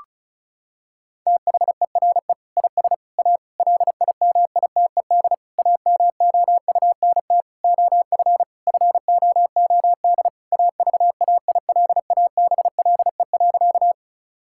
Morse code, heard as a keyed tone, about 36 wpm, 700 Hz. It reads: THERE IS A LIMITED AMOUNT OF FOOD AVAILABLE.
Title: Sentences from Top 1000 Words - Review Part 6 - 35wpm